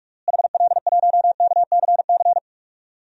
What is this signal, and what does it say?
Morse code, heard as a keyed tone, about 46 wpm, 700 Hz. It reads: HL1KCC
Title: DXCC-100 Call Signs - 6 Characters - Repeated 3x - 45wpm